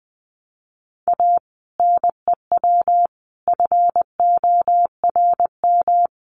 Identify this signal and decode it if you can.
Morse code, heard as a keyed tone, about 20 wpm, 700 Hz. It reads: A NEW FORM
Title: Sentences from Top 1300 Words 20wpm